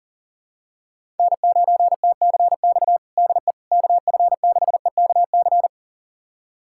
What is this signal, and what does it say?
Morse code, heard as a keyed tone, about 40 wpm, 700 Hz. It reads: N9TCX DE KF6EKC
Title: USA Callsign DE Callsign 40wpm